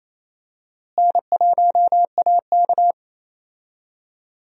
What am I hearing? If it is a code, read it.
Morse code, heard as a keyed tone, about 28 wpm, 700 Hz. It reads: N1AK
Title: USA Call Signs - 4 Characters - Repeated 3x - 28wpm